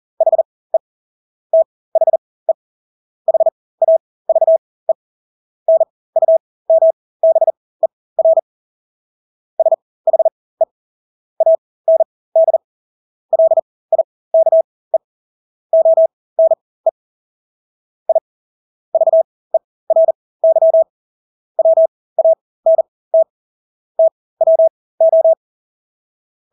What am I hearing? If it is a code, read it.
Morse code, heard as a keyed tone, about 40 wpm, spaced as for 20 wpm, 650 Hz. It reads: HE THE HAVE NUMBER SHE AND LIKE ONE I VERY WANT TWO
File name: Engelske ord 40-20 wpm